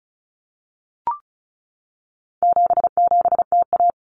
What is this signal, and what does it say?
Morse code, heard as a keyed tone, about 34 wpm, 700 Hz. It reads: 77TA
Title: Common Cut Numbers - 4-Digit 35wpm